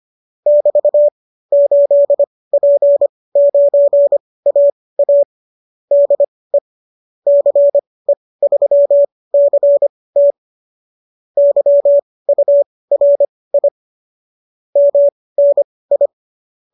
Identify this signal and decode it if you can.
Morse code, heard as a keyed tone, about 25 wpm, spaced as for 18 wpm, 590 Hz. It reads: <BT> 8P9AA DE CE3CT YURI MNI